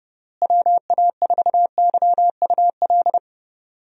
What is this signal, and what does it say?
Morse code, heard as a keyed tone, about 29 wpm, 700 Hz. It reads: WA4YUL
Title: USA Call Signs - 6 Characters - Repeated 3x - 30wpm